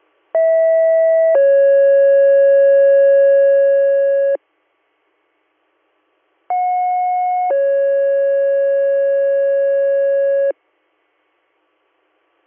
Two-tone page, heard as a one-second tone, then a three-second tone, about 650 then 550 Hz, about 730 then 550 Hz. It is heard from 0.3 to 10.5 s.